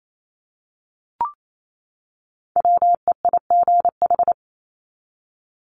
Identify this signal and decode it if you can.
Morse code, heard as a keyed tone, about 28 wpm, 700 Hz. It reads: WEIGH
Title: Top 1751-2000 Words 28wpm